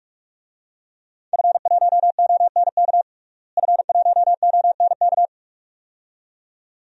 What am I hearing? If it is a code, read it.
Morse code, heard as a keyed tone, about 46 wpm, 700 Hz. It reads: F1ONK F1ONK
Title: DXCC-100 Call Signs - 5 Characters - Repeated 2x - 45wpm